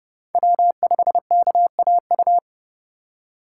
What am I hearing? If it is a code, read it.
Morse code, heard as a keyed tone, about 32 wpm, 700 Hz. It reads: W5KAU
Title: USA Call Signs - 5 Characters - 3x Character Spacing - 30wpm